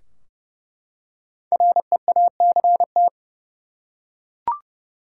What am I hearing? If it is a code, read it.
Morse code, heard as a keyed tone, about 32 wpm, 700 Hz. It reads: REACT
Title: Five Letter Words - 30wpm